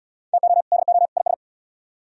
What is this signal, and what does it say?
Morse code, heard as a keyed tone, about 70 wpm, 700 Hz. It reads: TO DO IS